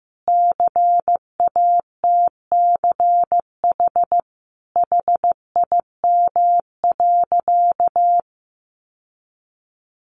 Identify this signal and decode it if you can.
Morse code, heard as a keyed tone, about 15 wpm, 700 Hz. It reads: CATCH HIM.
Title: Sentences from Top 1500 Words 15wpm